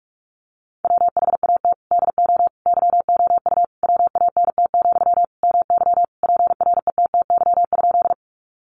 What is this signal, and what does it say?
Morse code, heard as a keyed tone, about 44 wpm, 700 Hz. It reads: WHAT DO YOU WANT, MY PRETTY?
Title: Sentences from Top 1000 Words 45wpm